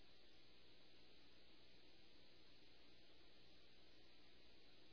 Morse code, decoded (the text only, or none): none